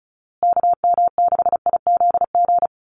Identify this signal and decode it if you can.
Morse code, heard as a keyed tone, about 34 wpm, 700 Hz. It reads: KM6IZG
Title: USA Call Signs Repeated 3x 35wpm